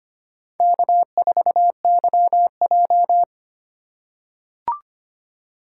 Morse code, heard as a keyed tone, about 25 wpm, 700 Hz. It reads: K4YJ
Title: USA Call Signs - 4x Character Spacing - 25wpm